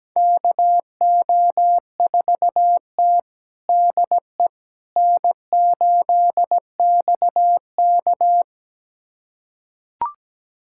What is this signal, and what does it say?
Morse code, heard as a keyed tone, about 17 wpm, 700 Hz. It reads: KO4T DE N8XK
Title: USA Callsign DE Callsign - 4 Characters - 17wpm